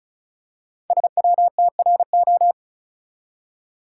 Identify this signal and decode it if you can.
Morse code, heard as a keyed tone, about 36 wpm, 700 Hz. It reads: SWTRO
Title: Code Groups - 5 Letters - 35wpm